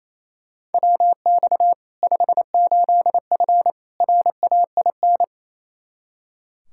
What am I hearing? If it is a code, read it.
Morse code, heard as a keyed tone, about 28 wpm, 700 Hz. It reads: WX 58F RAIN
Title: Practice QSO - Exchange 2 - Info - 28wpm